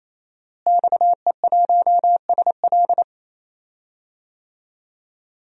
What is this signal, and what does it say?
Morse code, heard as a keyed tone, about 27 wpm, 700 Hz. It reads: XE1SL
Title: DXCC-100 Call Signs - 5 Characters - Repeated 3x - 28wpm